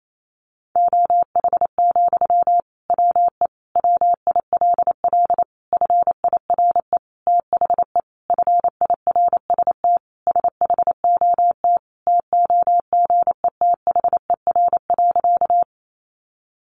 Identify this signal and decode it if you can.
Morse code, heard as a keyed tone, about 28 wpm, 700 Hz. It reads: OH, WE WILL FIRE THE FIRST SHOT TOGETHER.
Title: Sentences from Top 1100 Words 28wpm